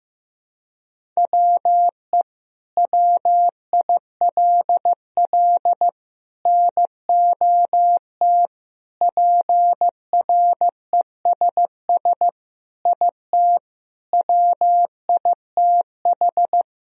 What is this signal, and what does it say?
Morse code, heard as a keyed tone, about 15 wpm, 700 Hz. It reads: WE WILL NOT PRESS IT WITH
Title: Sentences from Top 1000 Words - Review Part 1 - 15wpm